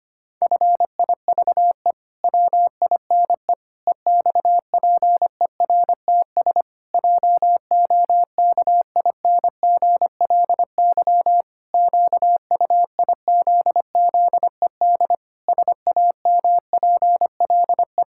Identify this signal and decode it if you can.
Morse code, heard as a keyed tone, about 25 wpm, 700 Hz. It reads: FIVE WINE EXPERTS JOKINGLY QUIZZED SAMPLE
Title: Pangram Challenge v2 - 25wpm